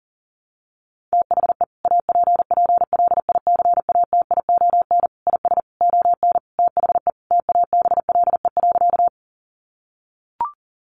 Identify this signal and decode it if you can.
Morse code, heard as a keyed tone, about 40 wpm, 700 Hz. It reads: THE APPLICATION IS ON THE TABLE.
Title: Sentences from Top 1200 Words 40wpm